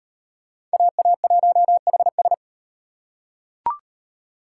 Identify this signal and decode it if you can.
Morse code, heard as a keyed tone, about 37 wpm, 700 Hz. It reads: AA1HS